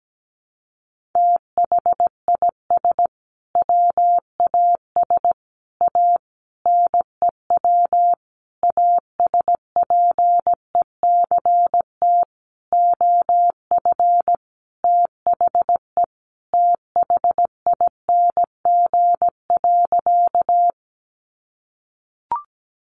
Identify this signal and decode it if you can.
Morse code, heard as a keyed tone, about 17 wpm, 700 Hz. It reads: THIS WAS A NEW ASPECT OF THE THING.
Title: Sentences from Top 1300 Words 17wpm